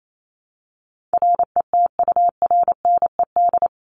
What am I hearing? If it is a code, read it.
Morse code, heard as a keyed tone, about 28 wpm, 700 Hz. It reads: RETURNED
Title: ED Suffix - Top 100 Words with Base Words 28wpm